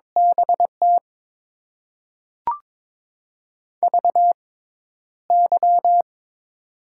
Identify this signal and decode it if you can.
Morse code, heard as a keyed tone, about 22 wpm, spaced as for 11 wpm, 700 Hz. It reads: BT VY
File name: Top 90 QSO Elements - 6x Character Spacing - 22wpm